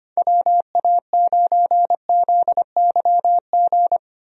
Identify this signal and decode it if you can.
Morse code, heard as a keyed tone, about 25 wpm, 700 Hz. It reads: WA9ZYG
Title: USA Call Signs - 6 Characters - 25wpm